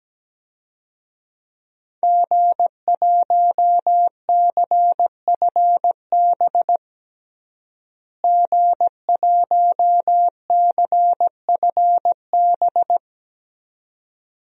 Morse code, heard as a keyed tone, about 17 wpm, 700 Hz. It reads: G1CFB G1CFB
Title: DXCC-100 Call Signs - 5 Characters - Repeated 2x - 17wpm